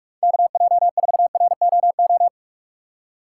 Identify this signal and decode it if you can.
Morse code, heard as a keyed tone, about 46 wpm, 700 Hz. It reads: KJ4ROO
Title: USA Call Signs - 6 Characters - 3x Character Spacing - 45wpm